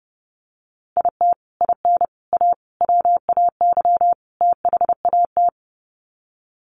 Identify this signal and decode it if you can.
Morse code, heard as a keyed tone, about 28 wpm, 700 Hz. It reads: IT IN A WAY THAT
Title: Sets of 5 Words Encore2 30wpm